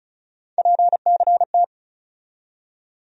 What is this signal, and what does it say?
Morse code, heard as a keyed tone, about 36 wpm, 700 Hz. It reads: PCT
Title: USA Call Sign Postfixes - 3 Letters 35wpm